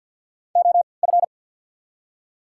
Morse code, heard as a keyed tone, about 48 wpm, spaced as for 28 wpm, 700 Hz. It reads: OF
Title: 3x Character Spacing - Top 100 Words - 50wpm